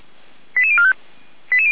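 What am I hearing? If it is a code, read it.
Five-tone selective call (ZVEI), heard as quick runs of five tones, tones of about 70 ms. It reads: 88045
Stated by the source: Five-tone ZVEI selective call received on SDR, from the audio server of Feuerwehr Orlamünde (fire brigade).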